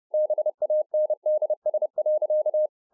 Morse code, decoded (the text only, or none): BANDS.